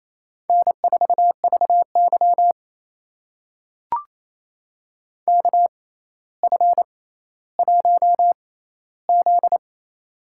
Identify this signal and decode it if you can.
Morse code, heard as a keyed tone, about 28 wpm, spaced as for 14 wpm, 700 Hz. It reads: N4VY KF1Z
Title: USA Call Signs - 4 Characters - 6x Character Spacing - 28wpm